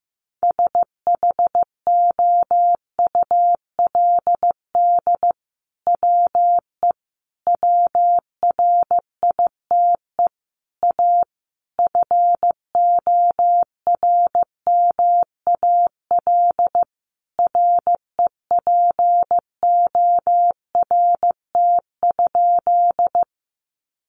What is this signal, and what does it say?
Morse code, heard as a keyed tone, about 15 wpm, 700 Hz. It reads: SHOULD WE WRITE A FORMAL REPORT?
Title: Sentences from Top 1300 Words 15wpm